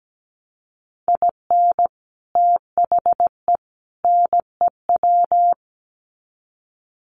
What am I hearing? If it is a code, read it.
Morse code, heard as a keyed tone, about 17 wpm, 700 Hz. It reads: IN THE NEW
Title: Sets of 3 Words 17wpm